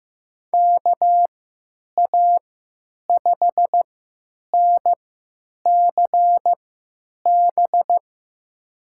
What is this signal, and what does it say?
Morse code, heard as a keyed tone, about 15 wpm, spaced as for 9 wpm, 700 Hz. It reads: KA5NCB
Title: USA Call Signs - 6 Characters - 3x Character Spacing - 15wpm